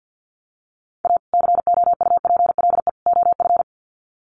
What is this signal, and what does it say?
Morse code, heard as a keyed tone, about 49 wpm, 700 Hz. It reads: A COUPLE OF
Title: Sets of 3 Words 50wpm